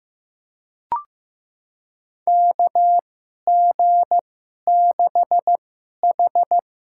Morse code, heard as a keyed tone, about 15 wpm, spaced as for 11 wpm, 700 Hz. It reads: KG6H
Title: USA Call Signs - 6 Characters - 2x Character Spacing - 15wpm